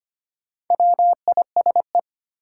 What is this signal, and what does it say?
Morse code, heard as a keyed tone, about 25 wpm, 700 Hz. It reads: WISE
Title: Sentences from Top 1600 Words - New Word Drill - 25wpm